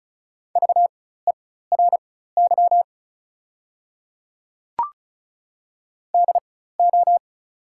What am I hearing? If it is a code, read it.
Morse code, heard as a keyed tone, about 34 wpm, spaced as for 16 wpm, 690 Hz. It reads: VERY DO
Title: 4x Character Spacing - Top 200 Words - 35wpm